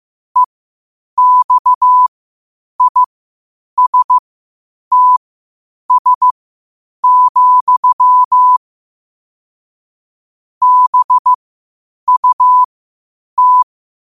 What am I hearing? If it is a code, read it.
Morse code, heard as a keyed tone, about 15 wpm, spaced as for 8 wpm, 1000 Hz. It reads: EXISTS, BUT
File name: qotdfiles / 8wpm